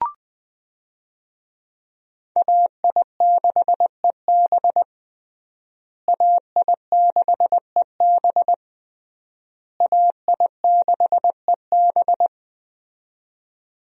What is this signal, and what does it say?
Morse code, heard as a keyed tone, about 20 wpm, 700 Hz. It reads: AI6EB AI6EB AI6EB
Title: USA Call Signs - 5 Characters - Repeated 3x - 20wpm